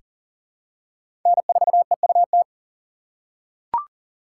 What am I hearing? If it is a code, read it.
Morse code, heard as a keyed tone, about 40 wpm, 700 Hz. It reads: N4EUT
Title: USA Call Signs - 5 Characters - 2x Character Spacing - 40wpm